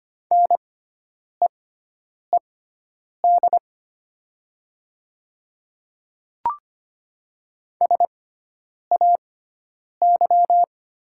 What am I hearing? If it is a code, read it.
Morse code, heard as a keyed tone, about 25 wpm, spaced as for 9 wpm, 700 Hz. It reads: NEED SAY
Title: 6x Character Spacing - Top 300 Words - 25wpm